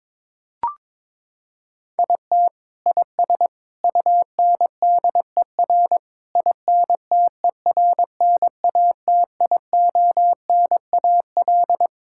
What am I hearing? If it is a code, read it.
Morse code, heard as a keyed tone, about 22 wpm, 700 Hz. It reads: IT IS UNDER INTERNATIONAL
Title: Sentences from Top 1000 Words - Review Part 6 - 22wpm